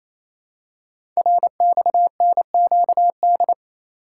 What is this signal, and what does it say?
Morse code, heard as a keyed tone, about 27 wpm, 700 Hz. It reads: RXNQD
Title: Code Groups - 5 Letters - 28wpm